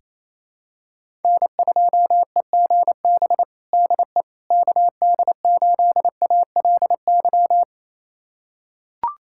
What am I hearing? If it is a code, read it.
Morse code, heard as a keyed tone, about 28 wpm, 700 Hz. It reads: N2EGB DE KD8ALY
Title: USA Callsign DE Callsign 28wpm